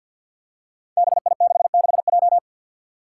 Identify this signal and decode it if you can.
Morse code, heard as a keyed tone, about 53 wpm, 700 Hz. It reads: BI66J
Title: Five Challenging Letter-Number Combinations 50wpm